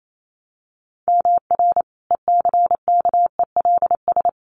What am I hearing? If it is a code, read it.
Morse code, heard as a keyed tone, about 28 wpm, 700 Hz. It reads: MR ECKELS
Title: A Sound Of Thunder 28wpm